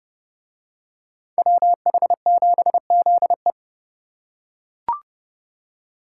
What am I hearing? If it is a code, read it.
Morse code, heard as a keyed tone, about 29 wpm, 700 Hz. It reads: WH7ZE